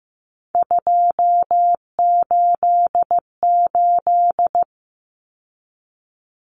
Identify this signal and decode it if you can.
Morse code, heard as a keyed tone, about 15 wpm, 700 Hz. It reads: 288
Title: Common Cut Numbers - 3-Digit 15wpm